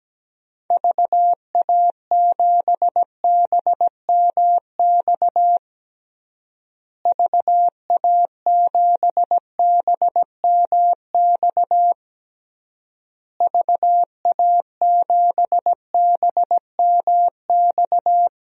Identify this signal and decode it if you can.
Morse code, heard as a keyed tone, about 17 wpm, 700 Hz. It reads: VA7BMX VA7BMX VA7BMX
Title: DXCC-100 Call Signs - 6 Characters - Repeated 3x - 17wpm